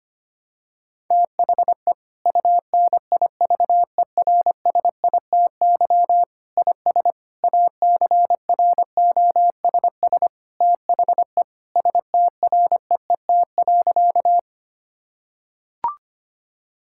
Morse code, heard as a keyed tone, about 25 wpm, 700 Hz. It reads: THE UNIVERSITY IS ACROSS THE STREET.
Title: Sentences from Top 1000 Words - Review Part 5 - 25wpm